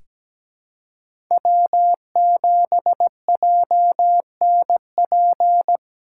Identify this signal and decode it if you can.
Morse code, heard as a keyed tone, about 17 wpm, 700 Hz. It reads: W7JNP
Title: USA Callsign DE Callsign 17wpm